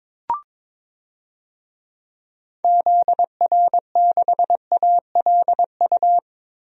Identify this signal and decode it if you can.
Morse code, heard as a keyed tone, about 22 wpm, 700 Hz. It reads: ZR6ALU
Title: DXCC-100 Call Signs - 6 Characters - Repeated 3x - 22wpm